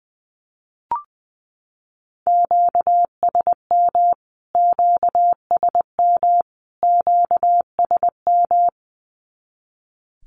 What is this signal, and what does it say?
Morse code, heard as a keyed tone, about 20 wpm, 700 Hz. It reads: QSM QSM QSM
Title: Q-Codes Repeated 3x 20wpm